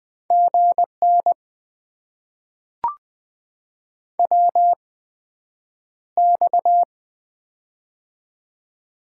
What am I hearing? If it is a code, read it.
Morse code, heard as a keyed tone, about 20 wpm, spaced as for 9 wpm, 700 Hz. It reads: GN WX
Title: Top 90 QSO Elements - 8x Character Spacing - 20wpm